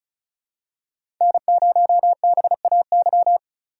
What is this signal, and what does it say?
Morse code, heard as a keyed tone, about 34 wpm, 700 Hz. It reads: N0BAY
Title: USA Call Signs - 5 Characters - 3x Character Spacing - 35wpm